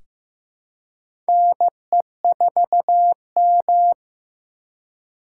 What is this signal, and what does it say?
Morse code, heard as a keyed tone, about 15 wpm, 700 Hz. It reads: NE4M
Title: K1USN SST QSO Practice - 15wpm